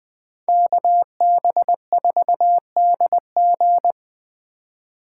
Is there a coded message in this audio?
Yes: Morse code from 0.5 to 3.9 s, heard as a keyed tone, about 20 wpm, 700 Hz. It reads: KB4DG